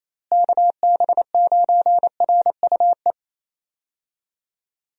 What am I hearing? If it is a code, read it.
Morse code, heard as a keyed tone, about 28 wpm, 700 Hz. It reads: KB9RUE
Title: USA Call Signs - 6 Characters - Repeated 3x - 28wpm